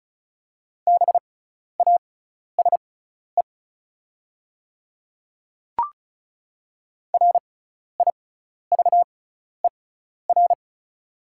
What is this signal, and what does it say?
Morse code, heard as a keyed tone, about 34 wpm, spaced as for 12 wpm, 700 Hz. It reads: BASE RIVER
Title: 6x Character Spacing - Top 300 Words - 35wpm